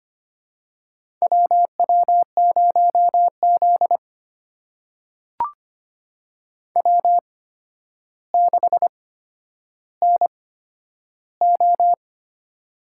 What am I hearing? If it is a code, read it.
Morse code, heard as a keyed tone, about 25 wpm, spaced as for 11 wpm, 700 Hz. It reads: WW0Z W6NO
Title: USA Call Signs - 4 Characters - 8x Character Spacing - 25wpm